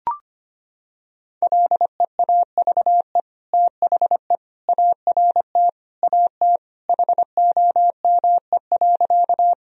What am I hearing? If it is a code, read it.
Morse code, heard as a keyed tone, about 25 wpm, 700 Hz. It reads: LEAVE THE ART AT HOME.
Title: Sentences from Top 1000 Words - Review Part 3 - 25wpm